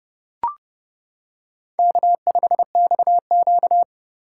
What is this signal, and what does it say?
Morse code, heard as a keyed tone, about 30 wpm, 700 Hz. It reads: K5XQ
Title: K1USN SST QSO Practice - 30wpm